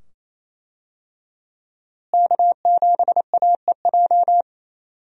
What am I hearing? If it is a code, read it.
Morse code, heard as a keyed tone, about 28 wpm, 700 Hz. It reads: K7AEJ
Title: USA Call Signs - 5 Characters - Repeated 2x - 28wpm